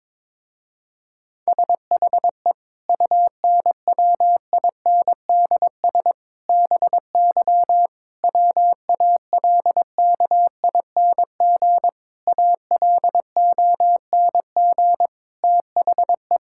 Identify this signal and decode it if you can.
Morse code, heard as a keyed tone, about 22 wpm, 700 Hz. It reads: SHE UNWINDS BY WALKING ALONG THE
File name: Difficult Sentences with Hints - 22wpm